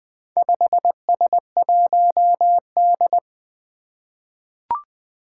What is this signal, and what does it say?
Morse code, heard as a keyed tone, about 20 wpm, 700 Hz. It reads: 5S1D
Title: Four Challenging Letter-Number Combinations 20wpm